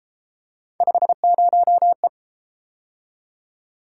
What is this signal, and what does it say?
Morse code, heard as a keyed tone, about 31 wpm, 700 Hz. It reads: 50E